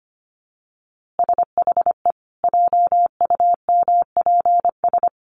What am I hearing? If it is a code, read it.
Morse code, heard as a keyed tone, about 25 wpm, 700 Hz. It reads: SHE JUMPS
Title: Easy Phrases 25wpm